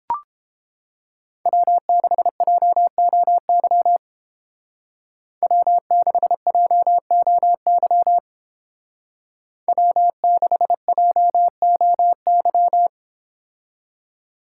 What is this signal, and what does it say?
Morse code, heard as a keyed tone, about 28 wpm, 700 Hz. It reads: W6JOY W6JOY W6JOY